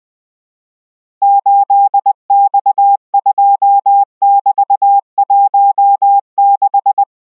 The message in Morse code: 8X2<BT>16